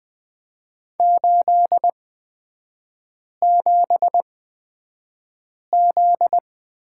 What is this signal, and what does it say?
Morse code, heard as a keyed tone, about 20 wpm, spaced as for 5 wpm, 700 Hz. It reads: 87Z
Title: 243 Lesson 39e - Letter z - Three Characters 20wpm